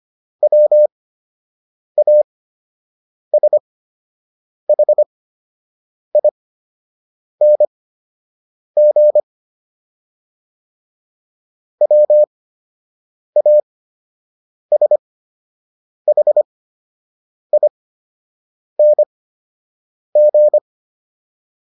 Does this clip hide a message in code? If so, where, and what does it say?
Yes: Morse code from 0.4 to 20.6 s, heard as a keyed tone, about 25 wpm, spaced as for 7 wpm, 600 Hz. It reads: WASHING WASHING